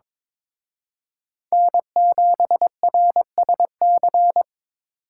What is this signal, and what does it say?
Morse code, heard as a keyed tone, about 26 wpm, 700 Hz. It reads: N7RSC
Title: Speed Racing - USA Call Signs 22wpm